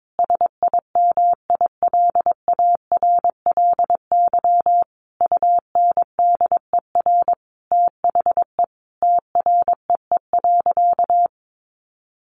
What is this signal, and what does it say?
Morse code, heard as a keyed tone, about 22 wpm, 700 Hz. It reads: SIMILARLY UNDER THE TREE.
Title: Sentences from Top 1400 Words 22wpm